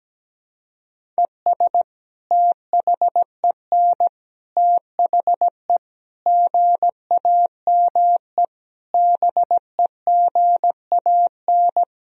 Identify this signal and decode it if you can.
Morse code, heard as a keyed tone, about 17 wpm, 700 Hz. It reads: ES THEN THE GAME BEGAN
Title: Sentences from Top 500 Words and Abbreviations 17wpm